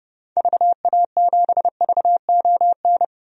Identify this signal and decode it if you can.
Morse code, heard as a keyed tone, about 31 wpm, 700 Hz. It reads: VA7VON
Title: DXCC-100 Call Signs - 6 Characters - 30wpm